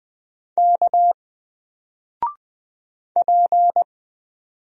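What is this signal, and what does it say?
Morse code, heard as a keyed tone, about 20 wpm, 700 Hz. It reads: K P
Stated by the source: Single letter and number Morse drill at 20 wpm, from N4LCD.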